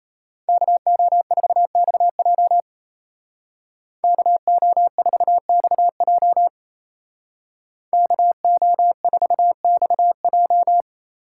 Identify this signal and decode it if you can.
Morse code, heard as a keyed tone, about 36 wpm, 700 Hz. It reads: KO4XJ KO4XJ KO4XJ